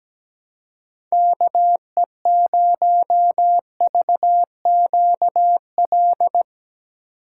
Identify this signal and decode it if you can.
Morse code, heard as a keyed tone, about 17 wpm, 700 Hz. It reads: KE0VQL
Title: USA Call Signs - 3x Character Spacing - 17wpm